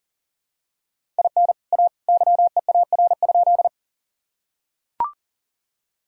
Morse code, heard as a keyed tone, about 40 wpm, 700 Hz. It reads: IN A YEAR?
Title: Sentences from Top 1000 Words - Review Part 1 - 40wpm